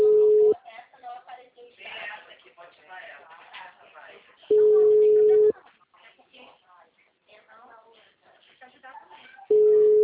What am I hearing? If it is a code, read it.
Ringback tone (Europe, 425 Hz), with speech.